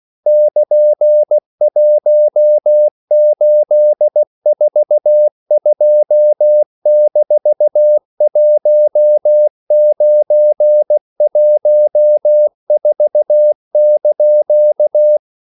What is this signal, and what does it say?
Morse code, heard as a keyed tone, about 16 wpm, 600 Hz. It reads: <KN>1842-1914)
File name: CW / Practice / EnglishQuotations0416